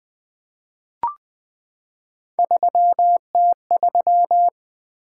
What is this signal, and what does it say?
Morse code, heard as a keyed tone, about 20 wpm, 700 Hz. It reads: 3T3
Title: Three Letter-Number Combinations - 20wpm